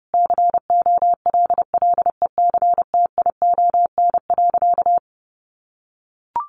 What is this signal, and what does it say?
Morse code, heard as a keyed tone, about 30 wpm, 700 Hz. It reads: COLLECTION.
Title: Sentences from Top 1200 Words 30wpm